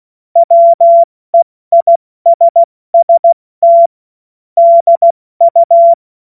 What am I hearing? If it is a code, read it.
Morse code, heard as a keyed tone, about 16 wpm, 680 Hz. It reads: WEISST DU
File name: Morsekurs / Witze / Witz 22. Februar 2022 70 F80